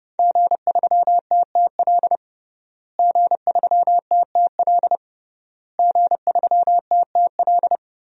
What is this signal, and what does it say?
Morse code, heard as a keyed tone, about 32 wpm, 700 Hz. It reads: G3TTL G3TTL G3TTL
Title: DXCC-100 Call Signs - 5 Characters - Repeated 3x - 30wpm